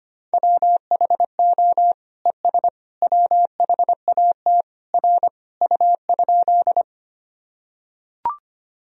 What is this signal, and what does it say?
Morse code, heard as a keyed tone, about 25 wpm, 700 Hz. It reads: WHO ES WHAT R U?
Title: Sentences from Top 100 Words and Abbreviations 25wpm